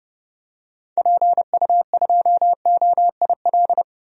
Morse code, heard as a keyed tone, about 31 wpm, 700 Hz. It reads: PU2OIL
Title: DXCC-100 Call Signs - 6 Characters - 30wpm